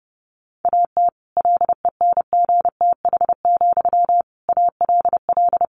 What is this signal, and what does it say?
Morse code, heard as a keyed tone, about 30 wpm, 700 Hz. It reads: AT LENGTH, ALL
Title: Sentences from Top 1000 Words 30wpm